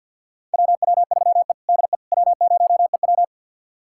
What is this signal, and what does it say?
Morse code, heard as a keyed tone, about 52 wpm, 700 Hz. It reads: WW3E DE W0EW
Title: USA Callsign DE Callsign - 4 Characters - 50wpm